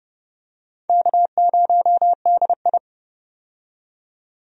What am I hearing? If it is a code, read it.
Morse code, heard as a keyed tone, about 30 wpm, 700 Hz. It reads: K0DI
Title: USA Call Signs - 4 Characters - 30wpm